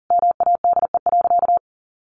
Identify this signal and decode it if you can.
Morse code, heard as a keyed tone, about 40 wpm, 700 Hz. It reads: MADE.
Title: Sentences from Top 1500 Words 40wpm